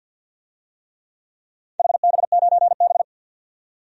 Morse code, heard as a keyed tone, about 47 wpm, 700 Hz. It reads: HB9B
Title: DXCC-100 Call Signs - 4 Characters - 50wpm